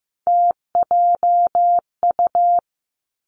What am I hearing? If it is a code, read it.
Morse code, heard as a keyed tone, about 15 wpm, 700 Hz. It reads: TJU
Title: Three Challenging Letter-Number Combinations 15wpm